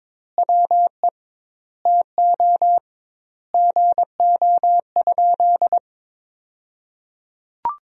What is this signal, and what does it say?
Morse code, heard as a keyed tone, about 21 wpm, 700 Hz. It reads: WE TO GO?
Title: 2x Word-Spacing - Sentences from Top 200 Words - 22wpm